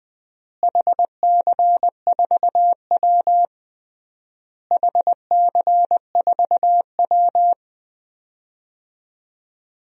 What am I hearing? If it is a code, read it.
Morse code, heard as a keyed tone, about 21 wpm, 700 Hz. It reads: HC4W HC4W
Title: DXCC-100 Call Signs - 4 Characters - Repeated 3x - 20wpm